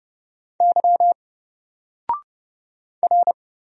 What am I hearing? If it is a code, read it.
Morse code, heard as a keyed tone, about 30 wpm, 700 Hz. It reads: Y R
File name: Single-Letter-Number-30wpm